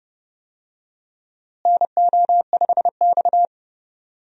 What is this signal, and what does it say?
Morse code, heard as a keyed tone, about 30 wpm, 700 Hz. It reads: NO5X
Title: USA Call Signs - 4 Characters - Repeated 2x - 30wpm